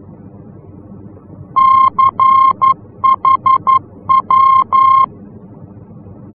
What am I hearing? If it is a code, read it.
Morse code, heard as a keyed tone, about 11 wpm, 1060 Hz. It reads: CHW
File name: france / mx / CHW-F-VOR